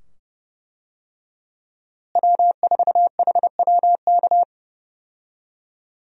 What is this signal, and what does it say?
Morse code, heard as a keyed tone, about 32 wpm, 700 Hz. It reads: W4HWK
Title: USA Call Signs - 5 Characters - Repeated 2x - 30wpm